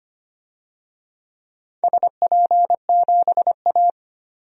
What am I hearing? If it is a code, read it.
Morse code, heard as a keyed tone, about 25 wpm, 700 Hz. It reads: SP7A
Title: DXCC-100 Call Signs - 4 Characters - 25wpm